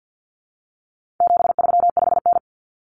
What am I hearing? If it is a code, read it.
Morse code, heard as a keyed tone, about 50 wpm, 700 Hz. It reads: 735N
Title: Common Cut Numbers - 4-Digit 50wpm